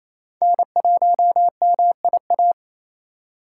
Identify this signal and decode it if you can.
Morse code, heard as a keyed tone, about 27 wpm, 700 Hz. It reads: N1MIA
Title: USA Call Signs - 3x Character Spacing - 28wpm